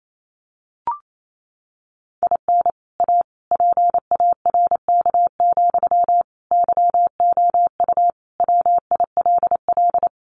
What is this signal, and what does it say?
Morse code, heard as a keyed tone, about 28 wpm, 700 Hz. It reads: IN A PARK, YOU WILL
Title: Sentences from Top 1300 Words 28wpm